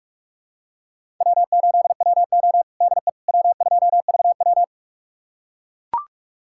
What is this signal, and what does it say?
Morse code, heard as a keyed tone, about 42 wpm, 700 Hz. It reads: W8WO DE W2VW
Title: USA Callsign DE Callsign - 4 Characters - 45wpm